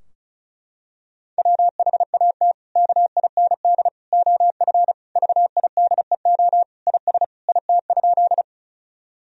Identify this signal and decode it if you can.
Morse code, heard as a keyed tone, about 36 wpm, 700 Hz. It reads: WHAT KIND OF VIDEO IS IT?
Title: Sentences from Top 1000 Words - Review Part 4 - 35wpm